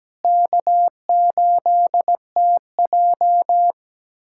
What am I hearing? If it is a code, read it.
Morse code, heard as a keyed tone, about 17 wpm, 700 Hz. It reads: K8TJ
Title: USA Callsign DE Callsign - 4 Characters - 17wpm